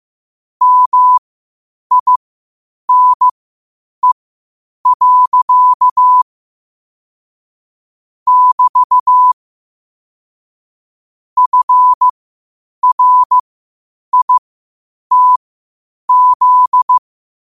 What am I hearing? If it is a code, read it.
Morse code, heard as a keyed tone, about 15 wpm, spaced as for 8 wpm, 1000 Hz. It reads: MINE. <BT> FRITZ
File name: qotdfiles / 8wpm